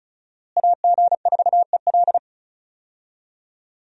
Morse code, heard as a keyed tone, about 34 wpm, 700 Hz. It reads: AG4EL
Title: USA Call Signs - 5 Characters - Repeated 3x - 35wpm